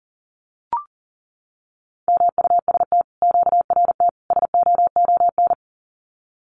Morse code, heard as a keyed tone, about 40 wpm, 700 Hz. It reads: MUST QRT SOON
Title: Common QSO Phrases Encore 40wpm